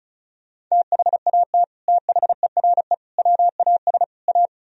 Morse code, heard as a keyed tone, about 37 wpm, 700 Hz. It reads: THAT THERE WAS A
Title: Speed-Racing - Sets of 4 Words 30wpm